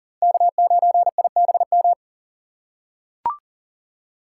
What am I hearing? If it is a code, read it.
Morse code, heard as a keyed tone, about 40 wpm, 700 Hz. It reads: K9IBM
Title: K1USN SST QSO Practice - 40wpm